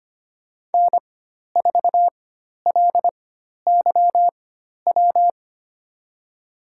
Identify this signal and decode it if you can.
Morse code, heard as a keyed tone, about 25 wpm, spaced as for 12 wpm, 700 Hz. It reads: N4LYW
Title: USA Call Signs - 5 Characters - 4x Character Spacing - 25wpm